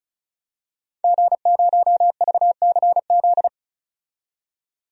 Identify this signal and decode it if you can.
Morse code, heard as a keyed tone, about 34 wpm, 700 Hz. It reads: G0VCZ
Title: DXCC-100 Call Signs - 5 Characters - 35wpm